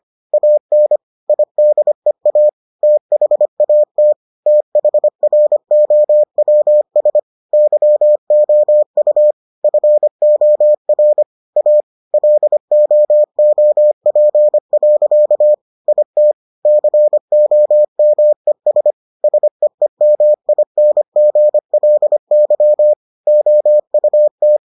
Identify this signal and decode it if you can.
Morse code, heard as a keyed tone, about 25 wpm, 600 Hz. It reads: AN IDEA THAT THROWS YOU FOR A LOOP. IT COMES SEEMINGLY OUT